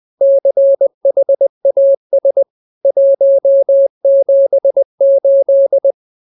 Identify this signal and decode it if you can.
Morse code, heard as a keyed tone, about 20 wpm, 550 Hz. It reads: CHAS 178